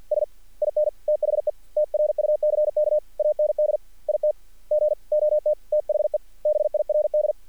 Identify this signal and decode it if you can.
Morse code, heard as a keyed tone, about 50 wpm, 600 Hz. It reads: S IN THE TRUCK AND IT GOT THE BIRD
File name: archived / CRSnet